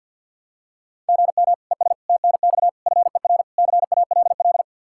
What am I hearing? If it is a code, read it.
Morse code, heard as a keyed tone, about 50 wpm, 700 Hz. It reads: GM ES TNX FER CALL
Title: K1USN SST QSO Practice 50wpm